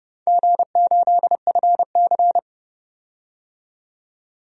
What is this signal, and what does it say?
Morse code, heard as a keyed tone, about 30 wpm, 700 Hz. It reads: G8FC
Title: DXCC-100 Call Signs - 4 Characters - 30wpm